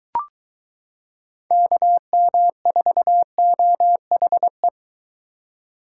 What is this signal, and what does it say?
Morse code, heard as a keyed tone, about 22 wpm, 700 Hz. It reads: KM4OHE